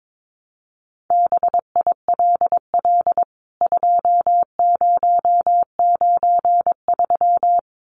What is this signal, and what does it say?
Morse code, heard as a keyed tone, about 22 wpm, 700 Hz. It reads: BILL 2093
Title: CWT Exchange 22wpm